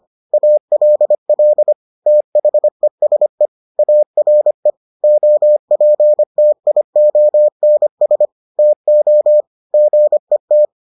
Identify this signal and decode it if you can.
Morse code, heard as a keyed tone, about 25 wpm, 600 Hz. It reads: ALL THESE ARE OPTIONS TO GET